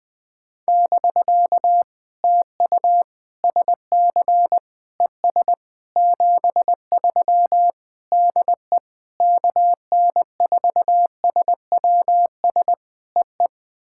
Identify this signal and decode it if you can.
Morse code, heard as a keyed tone, about 20 wpm, 700 Hz. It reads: <BK> TU SC ES 73 DE KN4SWS EE
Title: POTA QSO Practice - 20wpm